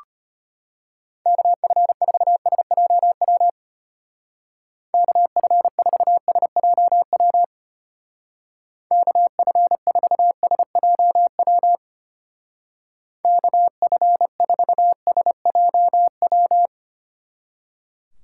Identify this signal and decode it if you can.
Morse code, heard as a keyed tone, about 31 wpm, 700 Hz. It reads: KF4SJW KF4SJW KF4SJW KF4SJW
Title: Speed Racing - USA Call Signs 25wpm